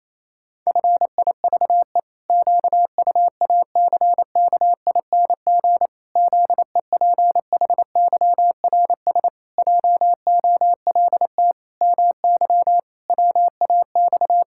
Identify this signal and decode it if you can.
Morse code, heard as a keyed tone, about 28 wpm, 700 Hz. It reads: FIVE QUACKING ZEPHYRS JOLT MY WAX
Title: Pangram Challenge v2 - 28wpm